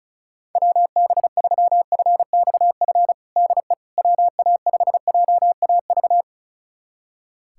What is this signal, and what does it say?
Morse code, heard as a keyed tone, about 34 wpm, 700 Hz. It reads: WB3FXF DE WA5JAV
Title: USA Callsign DE Callsign 35wpm